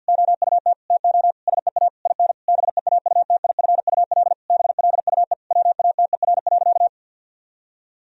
Morse code, heard as a keyed tone, about 50 wpm, 700 Hz. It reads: OUT TO SEA IN BEAUTIFUL BLUE WATER.
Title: Sentences from Top 1000 Words - Review Part 3 - 50wpm